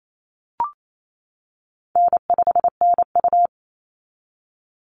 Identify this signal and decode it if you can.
Morse code, heard as a keyed tone, about 28 wpm, 700 Hz. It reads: N5NU
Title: Easy Call Signs - 28wpm